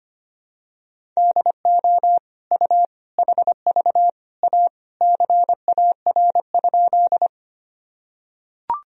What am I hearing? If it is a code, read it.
Morse code, heard as a keyed tone, about 25 wpm, 700 Hz. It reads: DO U HV A CAR?
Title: Sentences from Top 300 Words and Abbreviations 25wpm